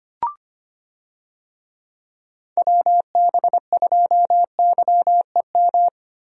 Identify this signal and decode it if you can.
Morse code, heard as a keyed tone, about 25 wpm, 700 Hz. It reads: WB2YEM
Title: USA Call Signs - 6 Characters - Repeated 3x - 25wpm